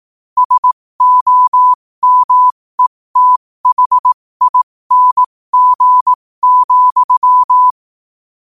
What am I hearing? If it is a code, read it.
Morse code, heard as a keyed tone, about 18 wpm, spaced as for 15 wpm, 1000 Hz. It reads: SOMETHING,
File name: qotdfiles / 15wpm